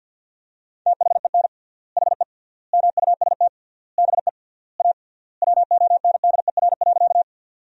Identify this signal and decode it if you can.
Morse code, heard as a keyed tone, about 50 wpm, 700 Hz. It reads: THEN HE MUST BE A WONDER.
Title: 3x Word-Spacing - Sentences from Top 500 Words - 50wpm